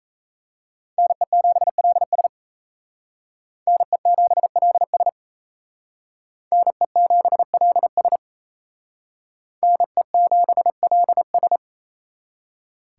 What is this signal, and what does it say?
Morse code, heard as a keyed tone, about 36 wpm, 700 Hz. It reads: NE7LS NE7LS NE7LS NE7LS